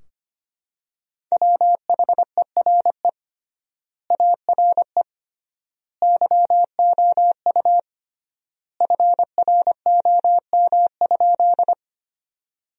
Morse code, heard as a keyed tone, about 25 wpm, 700 Hz. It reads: WHERE ARE YOU FROM?
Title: 3x Word-Spacing - Sentences from Top 200 Words - 25wpm